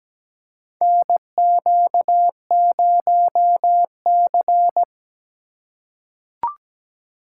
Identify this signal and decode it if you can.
Morse code, heard as a keyed tone, about 17 wpm, 700 Hz. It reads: NQ0C
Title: USA Call Signs - 4 Characters - 4x Character Spacing - 17wpm